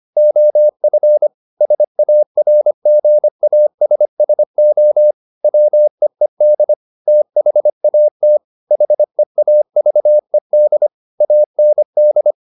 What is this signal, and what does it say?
Morse code, heard as a keyed tone, about 25 wpm, 600 Hz. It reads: OF SARGASSO WEED THAT HEAVED AND